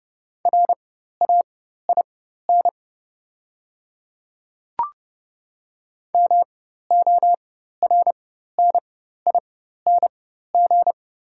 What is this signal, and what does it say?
Morse code, heard as a keyed tone, about 30 wpm, spaced as for 14 wpm, 690 Hz. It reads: RAIN MORNING
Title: 4x Character Spacing - Top 500 Words - 30wpm